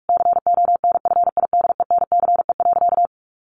Morse code, heard as a keyed tone, about 44 wpm, 700 Hz. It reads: CONFIDENCE.
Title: Sentences from Top 1500 Words 45wpm